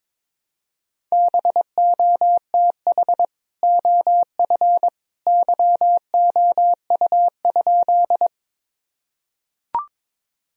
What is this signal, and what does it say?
Morse code, heard as a keyed tone, about 22 wpm, 700 Hz. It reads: BOTH OF YOU?
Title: Sentences from Top 1000 Words - Review Part 1 - 22wpm